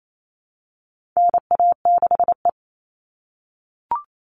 Morse code, heard as a keyed tone, about 28 wpm, 700 Hz. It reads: NA6E
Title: All Cut Numbers - 4-Digit 28wpm